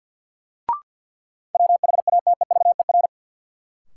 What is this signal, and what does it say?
Morse code, heard as a keyed tone, about 51 wpm, 700 Hz. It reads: WHATEVER
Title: Sentences from Top 1100 Words - New Word Drill - 50wpm